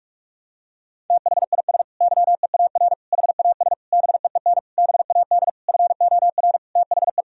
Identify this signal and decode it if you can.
Morse code, heard as a keyed tone, about 46 wpm, 700 Hz. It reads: THIS YEAR HAS BEEN BAD FOR THE
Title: Sentences from Top 1000 Words - Review Part 3 - 45wpm